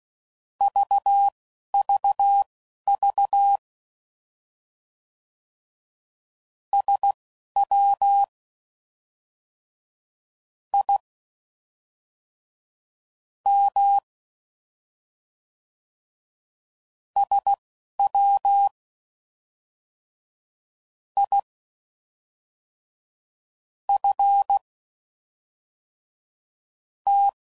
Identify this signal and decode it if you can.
Morse code, heard as a keyed tone, about 16 wpm, spaced as for 5 wpm, 800 Hz. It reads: VVV SWIM SWIFT